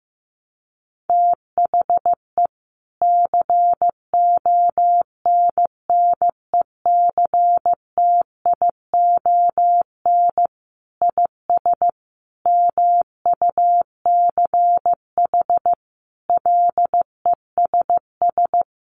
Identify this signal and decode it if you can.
Morse code, heard as a keyed tone, about 15 wpm, 700 Hz. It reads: THE CONNECTION IS MUCH LESS
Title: Sentences from Top 1300 Words 15wpm